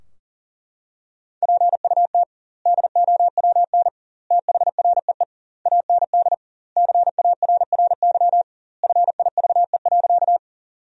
Morse code, heard as a keyed tone, about 40 wpm, 700 Hz. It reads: PUT DOWN THREE AND CARRY FIVE.
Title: 2x Word-Spacing - Sentences from Top 400 Words - 40wpm